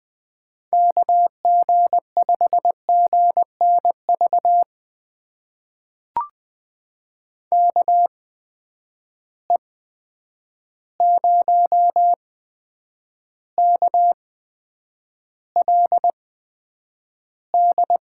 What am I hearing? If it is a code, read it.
Morse code, heard as a keyed tone, about 20 wpm, spaced as for 9 wpm, 700 Hz. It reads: KG5GNV KE0KLD
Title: USA Call Signs - 6 Characters - 8x Character Spacing - 20wpm